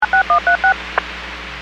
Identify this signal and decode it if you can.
DTMF, heard as telephone touch tones, tones of about 110 ms. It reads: 6136